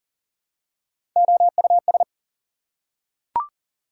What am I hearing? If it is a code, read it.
Morse code, heard as a keyed tone, about 40 wpm, 700 Hz. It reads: OUS
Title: Prefixes and Suffixes 40wpm